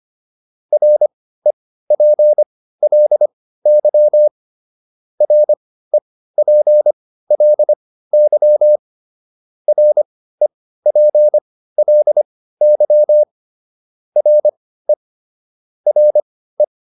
Morse code, heard as a keyed tone, about 25 wpm, spaced as for 15 wpm, 600 Hz. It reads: REPLY REPLY REPLY RE RE